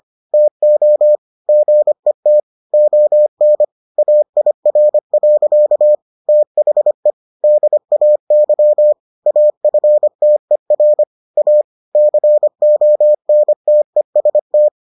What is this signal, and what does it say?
Morse code, heard as a keyed tone, about 25 wpm, 600 Hz. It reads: TO GET ON AIR. THE DAY AFTER A CONTEST